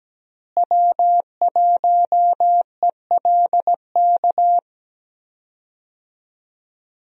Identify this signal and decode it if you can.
Morse code, heard as a keyed tone, about 17 wpm, 700 Hz. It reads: W1ELK
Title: USA Call Signs - 5 Characters - Repeated 3x - 17wpm